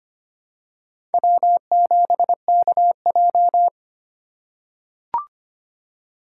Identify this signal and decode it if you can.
Morse code, heard as a keyed tone, about 25 wpm, 700 Hz. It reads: W7KJ
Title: USA Call Signs - 4 Characters - 3x Character Spacing - 25wpm